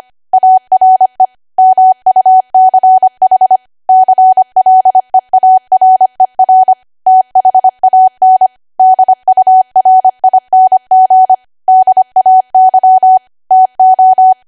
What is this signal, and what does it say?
Morse code, heard as a keyed tone, about 25 wpm, 750 Hz. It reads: ARE MUCH CLEARER THAN DURING DAY TO
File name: cwa / W1AW-Level3 / 25WPM